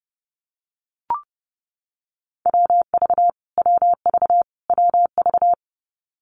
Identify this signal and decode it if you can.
Morse code, heard as a keyed tone, about 30 wpm, 700 Hz. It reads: WV WV WV
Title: US States Repeated 3x 30wpm